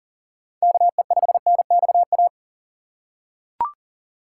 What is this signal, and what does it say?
Morse code, heard as a keyed tone, about 39 wpm, 700 Hz. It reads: KE5NXA
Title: USA Call Signs - 6 Characters - 2x Character Spacing - 40wpm